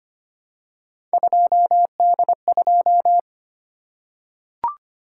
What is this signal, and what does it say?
Morse code, heard as a keyed tone, about 25 wpm, 700 Hz. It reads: 2D2
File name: Three Letter-Number Combinations - 25wpm